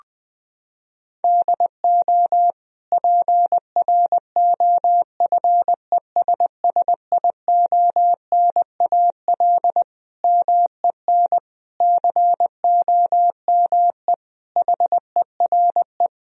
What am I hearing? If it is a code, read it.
Morse code, heard as a keyed tone, about 20 wpm, 700 Hz. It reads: DO PROFESSIONAL MEN COME HERE
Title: Sentences from Top 1000 Words - Review Part 4 - 20wpm